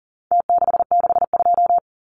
Morse code, heard as a keyed tone, about 40 wpm, 700 Hz. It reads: T662
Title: Numbers - Contest Style - 4-Digit 40wpm